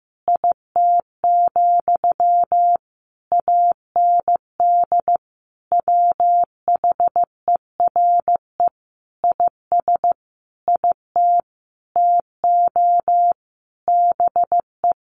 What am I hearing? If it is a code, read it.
Morse code, heard as a keyed tone, about 15 wpm, 700 Hz. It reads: IT, AND WHERE IS IT TO BE
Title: Sentences from Top 300 Words 15wpm